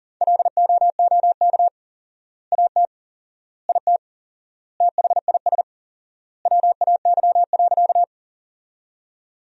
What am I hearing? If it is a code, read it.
Morse code, heard as a keyed tone, about 37 wpm, 700 Hz. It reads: LOOK AT IT THIS WAY.
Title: 4x Word-Spacing - Sentences from Top 100 Words - 40wpm